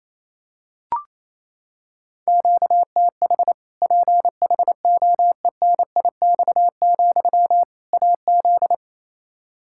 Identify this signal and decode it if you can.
Morse code, heard as a keyed tone, about 28 wpm, 700 Hz. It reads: QTH PHOENIX, AZ
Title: Practice QSO - Exchange 1 - Beginning - 28wpm